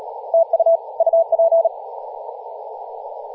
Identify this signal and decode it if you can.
Morse code, heard as a keyed tone, about 38 wpm, 660 Hz, with radio noise. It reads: TU UP
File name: posts / 2016-aug-26-2236z-yj0cow-18-073mhz-cw